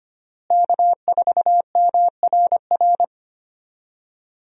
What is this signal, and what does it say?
Morse code, heard as a keyed tone, about 25 wpm, 700 Hz. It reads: K4MRR
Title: USA Call Signs - 5 Characters - Repeated 3x - 25wpm